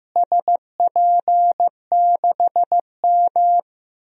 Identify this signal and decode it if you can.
Morse code, heard as a keyed tone, about 15 wpm, 700 Hz. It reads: SP6M
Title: DXCC-100 Call Signs - 4 Characters - Repeated 2x - 15wpm